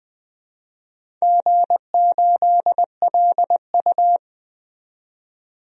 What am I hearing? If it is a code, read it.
Morse code, heard as a keyed tone, about 20 wpm, 700 Hz. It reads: G8LU
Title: DXCC-100 Call Signs - 4 Characters - 20wpm